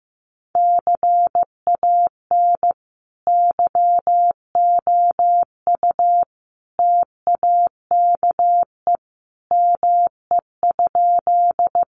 Morse code, heard as a keyed tone, about 15 wpm, 700 Hz. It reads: CAN YOU TAKE ME?
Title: Sentences from Top 200 Words 15wpm